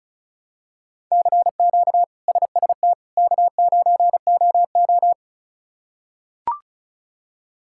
Morse code, heard as a keyed tone, about 36 wpm, 700 Hz. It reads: CQ SST K9OO
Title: K1USN SST QSO Practice - 35wpm